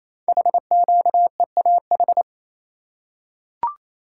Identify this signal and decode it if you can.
Morse code, heard as a keyed tone, about 28 wpm, 700 Hz. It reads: HQEAH
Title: Code Groups - 5 Letters - 28wpm